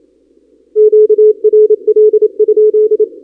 Morse code, heard as a keyed tone, about 28 wpm, 430 Hz. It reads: QRL?
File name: contests / 2019-cw-wpx-kt5j-k5pi- -2330 0-32